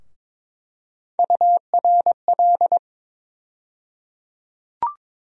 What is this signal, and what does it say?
Morse code, heard as a keyed tone, about 22 wpm, 700 Hz. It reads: URL
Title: USA Call Sign Postfixes - 3 Letters 22wpm